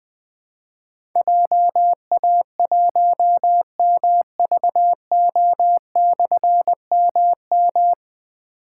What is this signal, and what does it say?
Morse code, heard as a keyed tone, about 20 wpm, 700 Hz. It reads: JA1MVO/MM
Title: Difficult Call Signs 20wpm